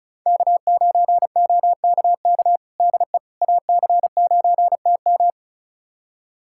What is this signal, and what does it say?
Morse code, heard as a keyed tone, about 36 wpm, 700 Hz. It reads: K9OKK DE AC9TM
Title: USA Callsign DE Callsign - 5 Characters - 35wpm